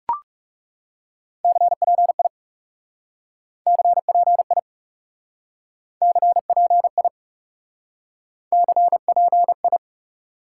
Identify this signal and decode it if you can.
Morse code, heard as a keyed tone, about 40 wpm, 700 Hz. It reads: CPI CPI CPI CPI